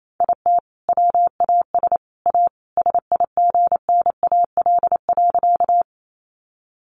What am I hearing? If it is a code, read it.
Morse code, heard as a keyed tone, about 28 wpm, 700 Hz. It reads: IT WAS A SIGNAL.
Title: Sentences from Top 1400 Words 28wpm